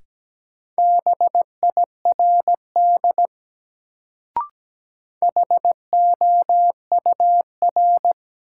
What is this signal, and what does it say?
Morse code, heard as a keyed tone, about 17 wpm, 700 Hz. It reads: BIRD HOUR
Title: Sentences from Top 400 Words - New Word Drill - 17wpm